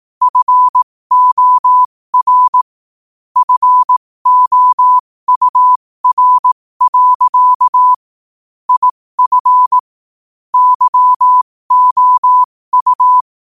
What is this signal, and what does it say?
Morse code, heard as a keyed tone, about 18 wpm, spaced as for 15 wpm, 1000 Hz. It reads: FOR FOUR. IF YOU
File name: qotdfiles / 15wpm